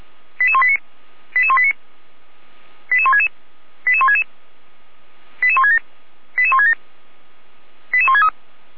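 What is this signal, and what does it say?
Five-tone selective call (ZVEI), heard as quick runs of five tones, tones of about 70 ms. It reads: 88189 88189 88166 88166 88167 88167 88163
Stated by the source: ZVEI five-tone call received by SDR, from the server of Feuerwehr Orlamünde.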